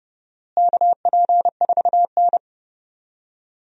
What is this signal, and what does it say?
Morse code, heard as a keyed tone, about 30 wpm, 700 Hz. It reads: KP4N
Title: DXCC-100 Call Signs - 4 Characters - Repeated 3x - 30wpm